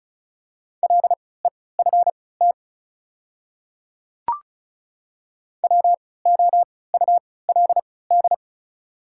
Morse code, heard as a keyed tone, about 36 wpm, spaced as for 20 wpm, 700 Hz. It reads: LEFT WOULD
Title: 3x Character Spacing - Top 300 Words - 35wpm